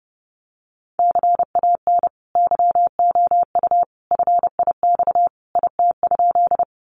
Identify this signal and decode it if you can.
Morse code, heard as a keyed tone, about 30 wpm, 700 Hz. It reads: CAN YOU FIX IT?
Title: Sentences from Top 1400 Words 30wpm